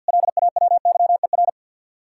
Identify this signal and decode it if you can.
Morse code, heard as a keyed tone, about 66 wpm, spaced as for 58 wpm, 700 Hz. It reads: LAWYER
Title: Speed-Racing Common QSO Phrases - 50wpm